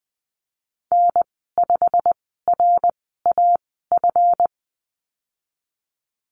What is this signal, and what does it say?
Morse code, heard as a keyed tone, about 19 wpm, spaced as for 14 wpm, 700 Hz. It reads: N5RAF
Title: 147 Lesson 24g - Letter y - Call Signs 20wpm